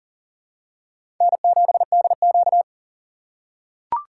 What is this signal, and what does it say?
Morse code, heard as a keyed tone, about 38 wpm, 700 Hz. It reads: N7DQ
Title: USA Call Signs - 4 Characters - 2x Character Spacing - 40wpm